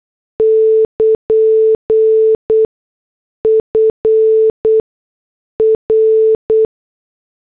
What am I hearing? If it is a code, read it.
Morse code, heard as a keyed tone, about 8 wpm, spaced as for 6 wpm, 440 Hz. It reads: <KN> FR
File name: morse / headlines